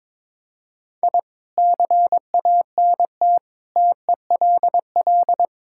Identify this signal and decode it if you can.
Morse code, heard as a keyed tone, about 22 wpm, 700 Hz. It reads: I CANT TELL
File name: practice / 22wpm / Sets of 6 Words - 22wpm